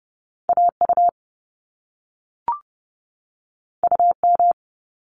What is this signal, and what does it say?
Morse code, heard as a keyed tone, about 30 wpm, 700 Hz. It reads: AU UM
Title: Top Two Letter Combinations in Words 30wpm